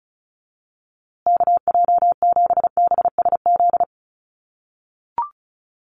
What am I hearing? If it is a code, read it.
Morse code, heard as a keyed tone, about 34 wpm, 700 Hz. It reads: KJ7BSZ
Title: USA Call Signs Repeated 2x 35wpm